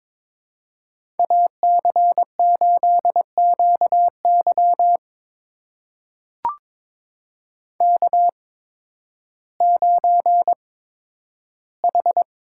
Morse code, heard as a keyed tone, about 22 wpm, spaced as for 12 wpm, 700 Hz. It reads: AC8QY K9H
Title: USA Call Signs - 8x Character Spacing - 22wpm